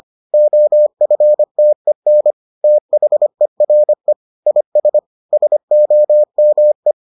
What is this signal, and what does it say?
Morse code, heard as a keyed tone, about 25 wpm, 600 Hz. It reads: OFTEN THERE IS SOME